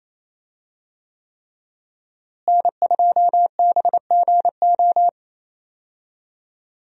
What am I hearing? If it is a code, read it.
Morse code, heard as a keyed tone, about 27 wpm, 700 Hz. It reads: N2BGO
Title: USA Call Signs - 5 Characters - Repeated 3x - 28wpm